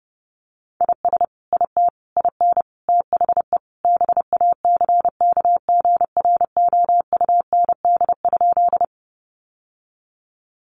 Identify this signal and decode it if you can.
Morse code, heard as a keyed tone, about 30 wpm, 700 Hz. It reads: IS IT IN THE BACKGROUND?
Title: Sentences from Top 1300 Words 30wpm